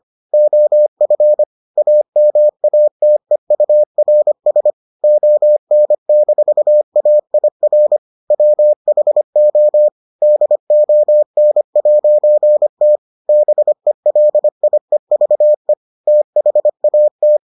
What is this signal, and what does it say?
Morse code, heard as a keyed tone, about 25 wpm, 600 Hz. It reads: OF AMATEURS ON-AIR WHO DON'T BELIEVE THAT